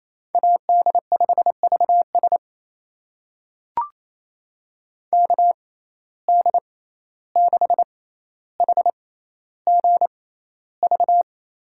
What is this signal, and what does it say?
Morse code, heard as a keyed tone, about 28 wpm, spaced as for 14 wpm, 700 Hz. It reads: AD5VS KD6HGV
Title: USA Call Signs - 6x Character Spacing - 28wpm